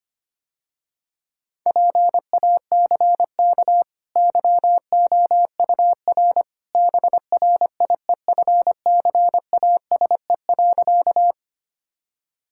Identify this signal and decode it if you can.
Morse code, heard as a keyed tone, about 25 wpm, 700 Hz. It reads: PACK YOUR BRIEFCASE.
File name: Easy Sentences with Hints - 25wpm